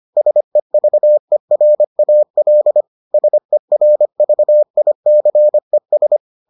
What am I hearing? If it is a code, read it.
Morse code, heard as a keyed tone, about 25 wpm, 600 Hz. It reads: SEVERAL SERVICES